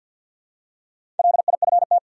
Morse code, heard as a keyed tone, about 52 wpm, 700 Hz. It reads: LIFT
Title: Four Letter Words - 50wpm